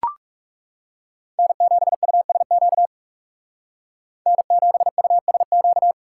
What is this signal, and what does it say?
Morse code, heard as a keyed tone, about 38 wpm, 700 Hz. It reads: N7USQ N7USQ